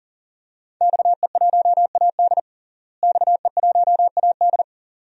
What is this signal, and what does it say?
Morse code, heard as a keyed tone, about 40 wpm, 700 Hz. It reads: XE1AD XE1AD
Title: DXCC-100 Call Signs - 5 Characters - Repeated 2x - 40wpm